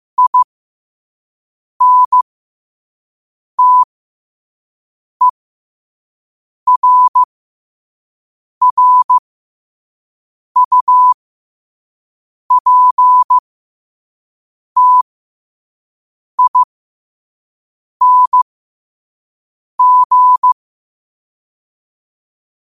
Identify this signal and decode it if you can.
Morse code, heard as a keyed tone, about 15 wpm, spaced as for 5 wpm, 1000 Hz. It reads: INTERRUPTING
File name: qotdfiles / 5wpm